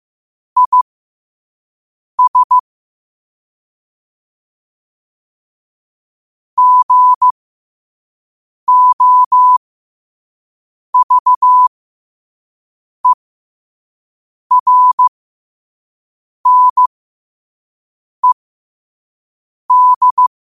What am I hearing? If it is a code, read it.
Morse code, heard as a keyed tone, about 15 wpm, spaced as for 5 wpm, 1000 Hz. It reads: IS GOVERNED
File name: qotdfiles / 5wpm